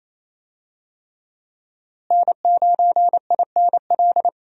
Morse code, heard as a keyed tone, about 28 wpm, 700 Hz. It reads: N9INL
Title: USA Call Signs - 5 Characters - Repeated 3x - 28wpm